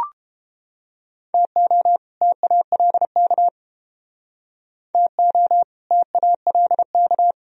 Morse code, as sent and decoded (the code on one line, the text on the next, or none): - --- / - .- .-.. -.- / - --- / - .- .-.. -.-
TO TALK TO TALK